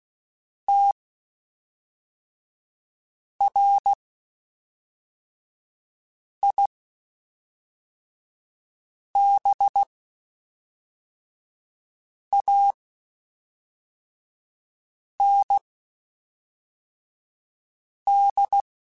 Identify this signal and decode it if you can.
Morse code, heard as a keyed tone, about 16 wpm, spaced as for 3 wpm, 800 Hz. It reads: TRIBAND
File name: QSOs / QSOs-mp3s / WD0BGP 1@5